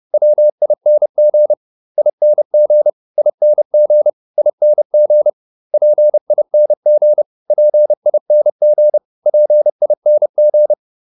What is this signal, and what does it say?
Morse code, heard as a keyed tone, about 30 wpm, 600 Hz. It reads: WING ING ING ING PING PING PING